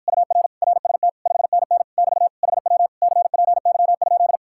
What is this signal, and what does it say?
Morse code, heard as a keyed tone, about 53 wpm, 700 Hz. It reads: UR RST 5NN <BT> HW CPY?